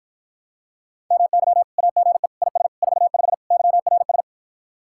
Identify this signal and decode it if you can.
Morse code, heard as a keyed tone, about 54 wpm, 700 Hz. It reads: MY AGE IS 45 YRS